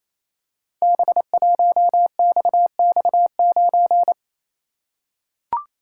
Morse code, heard as a keyed tone, about 28 wpm, 700 Hz. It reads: B1XX9
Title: Code Groups - 5 Letter:Numbers - 28wpm